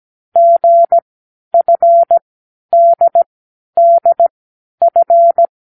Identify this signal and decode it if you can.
Morse code, heard as a keyed tone, about 17 wpm, spaced as for 10 wpm, 680 Hz. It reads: GFDDF
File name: k7qo / Lesson12(A-H)